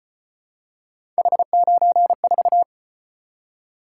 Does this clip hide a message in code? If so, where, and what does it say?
Yes: Morse code from 1.2 to 2.6 s, heard as a keyed tone, about 33 wpm, 700 Hz. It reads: H94